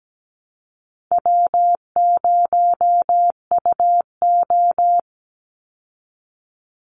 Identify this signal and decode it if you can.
Morse code, heard as a keyed tone, about 17 wpm, 700 Hz. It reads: W0UO
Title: Call Signs Repeated 2x - 17wpm